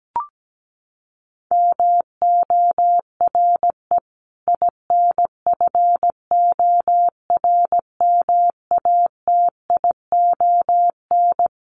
Morse code, heard as a keyed tone, about 17 wpm, 700 Hz. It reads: MORE INFORMATION
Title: Sentences from Top 1200 Words 17wpm